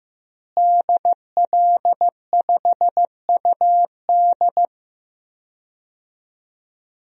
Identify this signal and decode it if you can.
Morse code, heard as a keyed tone, about 15 wpm, 700 Hz. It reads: DL5UD
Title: DXCC-100 Call Signs - 5 Characters - Repeated 2x - 15wpm